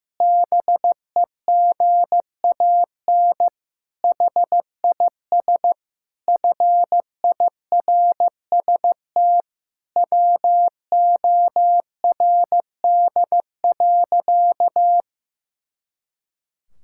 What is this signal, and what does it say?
Morse code, heard as a keyed tone, about 15 wpm, 700 Hz. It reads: BEGAN HIS FIRST WORD.
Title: Pangram Challenge v2 - 15wpm